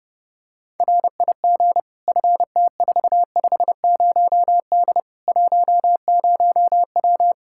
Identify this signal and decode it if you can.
Morse code, heard as a keyed tone, about 30 wpm, 700 Hz. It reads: RIG FT450D 10W
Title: K1USN SST QSO Practice 30wpm